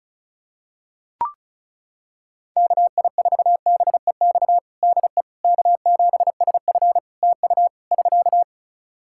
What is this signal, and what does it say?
Morse code, heard as a keyed tone, about 36 wpm, 700 Hz. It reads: KI4BEX DE K7SF TU <SK>
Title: Practice QSO - Exchange 4 - Ending - 35wpm